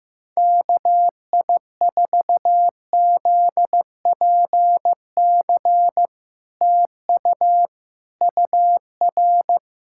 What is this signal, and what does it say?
Morse code, heard as a keyed tone, about 15 wpm, 700 Hz. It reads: KI4ZPC TU UR
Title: SOTA QSO Practice 15wpm